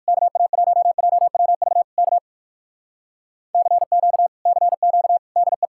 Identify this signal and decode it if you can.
Morse code, heard as a keyed tone, about 59 wpm, 700 Hz. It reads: KA1JWV K CQ CQ DE